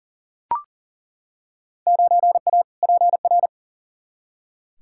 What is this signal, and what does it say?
Morse code, heard as a keyed tone, about 40 wpm, 700 Hz. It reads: 9A PR
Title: Field Day Exchanges Encore - 40wpm